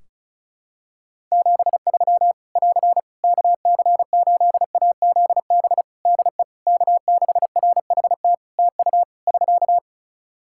Morse code, heard as a keyed tone, about 36 wpm, 700 Hz. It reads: 73 <AR> KC8AZB DE K6RHT TU <SK>
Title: Practice QSO - Exchange 4 - Ending - 35wpm